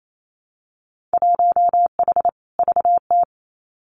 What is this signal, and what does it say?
Morse code, heard as a keyed tone, about 28 wpm, 700 Hz. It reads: 1H VT
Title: Winter Field Day Exchanges 28wpm